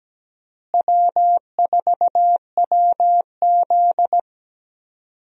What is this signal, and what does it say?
Morse code, heard as a keyed tone, about 17 wpm, 700 Hz. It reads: W4WZ
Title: USA Call Signs - 4 Characters - Repeated 3x - 17wpm